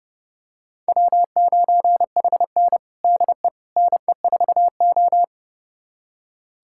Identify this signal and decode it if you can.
Morse code, heard as a keyed tone, about 32 wpm, 700 Hz. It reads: W9HN DE NE4O
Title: USA Callsign DE Callsign - 4 Characters - 30wpm